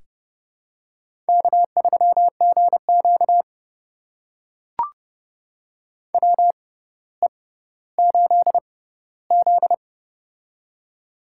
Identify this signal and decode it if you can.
Morse code, heard as a keyed tone, about 30 wpm, spaced as for 15 wpm, 700 Hz. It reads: K3GQ WE8Z
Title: USA Call Signs - 6x Character Spacing - 30wpm